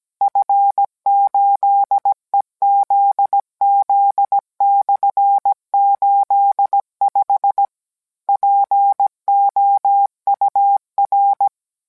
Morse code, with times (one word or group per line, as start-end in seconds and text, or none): 0.2-7.7 F8EZZ/85
8.3-11.5 POUR